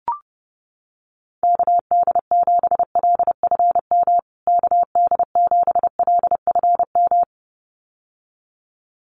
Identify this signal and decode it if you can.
Morse code, heard as a keyed tone, about 30 wpm, 700 Hz. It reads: KD7LFM KD7LFM
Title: USA Call Signs Repeated 2x 30wpm